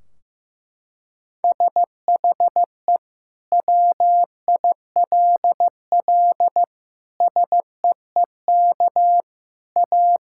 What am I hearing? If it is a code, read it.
Morse code, heard as a keyed tone, about 15 wpm, 700 Hz. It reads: SHE WILL SEEK A
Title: Sentences from Top 1600 Words 15wpm